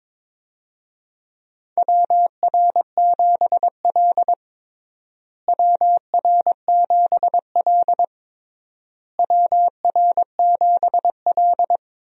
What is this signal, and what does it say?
Morse code, heard as a keyed tone, about 22 wpm, 700 Hz. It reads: WR7L WR7L WR7L
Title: USA Call Signs - 4 Characters - Repeated 3x - 22wpm